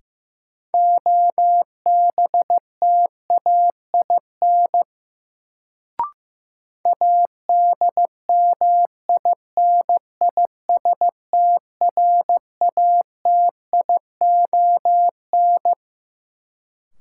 Morse code, heard as a keyed tone, about 15 wpm, 700 Hz. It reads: OBTAIN ADMINISTRATION
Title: Sentences from Top 1300 Words - New Word Drill Alt - 15wpm